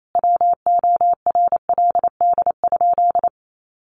Morse code, heard as a keyed tone, about 28 wpm, 700 Hz. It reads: WORLD?
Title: Sentences from Top 1400 Words 28wpm